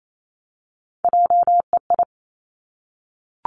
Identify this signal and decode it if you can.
Morse code, heard as a keyed tone, about 28 wpm, 690 Hz. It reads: JEI
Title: Three Challenging Letter-Number Combinations 28wpm